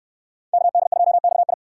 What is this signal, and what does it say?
Morse code, heard as a keyed tone, about 64 wpm, 700 Hz. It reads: KD2XI